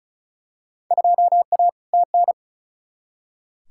Morse code, heard as a keyed tone, about 36 wpm, 700 Hz. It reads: 2A TN
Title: Field Day Exchanges - 35wpm